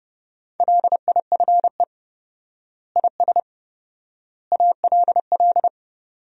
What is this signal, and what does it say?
Morse code, heard as a keyed tone, about 30 wpm, 700 Hz. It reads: LIFE IS ALL
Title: 4x Word-Spacing - Sentences from Top 300 Words - 30wpm